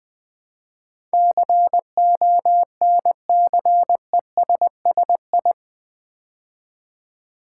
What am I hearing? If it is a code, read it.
Morse code, heard as a keyed tone, about 20 wpm, 700 Hz. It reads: CONCESSI
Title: Build A Word - 4 Letters At A Time - 10-Letter Words - 20wpm